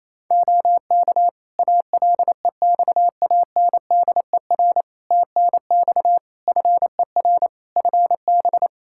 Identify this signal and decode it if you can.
Morse code, heard as a keyed tone, about 28 wpm, 700 Hz. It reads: OK ALEXANDER TNX FER FB
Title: K1USN SST QSO Practice 28wpm